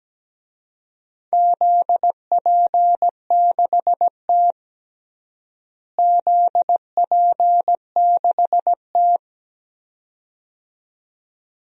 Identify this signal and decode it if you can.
Morse code, heard as a keyed tone, about 17 wpm, 700 Hz. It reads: ZP6T ZP6T
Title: DXCC-100 Call Signs - 4 Characters - Repeated 2x - 17wpm